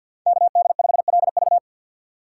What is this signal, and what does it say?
Morse code, heard as a keyed tone, about 53 wpm, 700 Hz. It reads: KD5LV
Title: USA Call Signs - 5 Characters - 6x Character Spacing - 50wpm